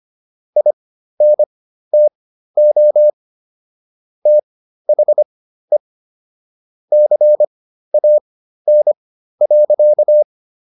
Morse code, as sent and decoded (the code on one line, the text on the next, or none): .. -. - --- / - .... / -.-. .- -. .-.-.-
INTO THE CAN.